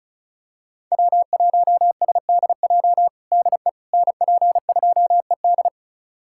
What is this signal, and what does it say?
Morse code, heard as a keyed tone, about 36 wpm, 700 Hz. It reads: W1SDJ DE NP2ED
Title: USA Callsign DE Callsign - 5 Characters - 35wpm